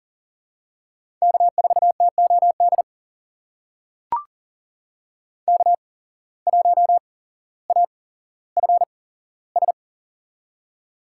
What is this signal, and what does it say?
Morse code, heard as a keyed tone, about 40 wpm, spaced as for 17 wpm, 700 Hz. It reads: K4TOD K1AFS
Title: USA Call Signs - 5 Characters - 8x Character Spacing - 40wpm